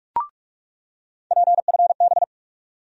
Morse code, heard as a keyed tone, about 46 wpm, 700 Hz. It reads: PFB
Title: USA Call Sign Postfixes - 3 Letters 45wpm